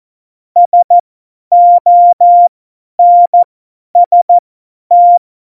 Morse code, heard as a keyed tone, about 14 wpm, spaced as for 10 wpm, 700 Hz. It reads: SONST
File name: Morsekurs / Witze / Witz 25. Juni 2019, 50 F70